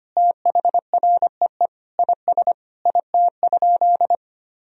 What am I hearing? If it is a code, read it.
Morse code, heard as a keyed tone, about 25 wpm, 700 Hz. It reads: THREE IS IT?
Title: Sentences from Top 1000 Words - Review Part 1 - 25wpm